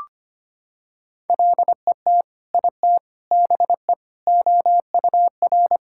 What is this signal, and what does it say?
Morse code, heard as a keyed tone, about 25 wpm, 700 Hz. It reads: LET IT BE OUR
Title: Sentences from Top 1600 Words 25wpm